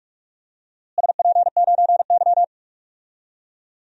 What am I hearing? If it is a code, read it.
Morse code, heard as a keyed tone, about 46 wpm, 700 Hz. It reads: SP9Y
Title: DXCC-100 Call Signs - 4 Characters - Repeated 2x - 45wpm